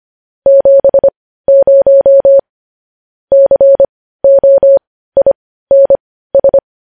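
Morse code, heard as a keyed tone, about 25 wpm, spaced as for 15 wpm, 550 Hz. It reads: 70 COINS
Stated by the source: Morse news headlines sent at 15 words per minute.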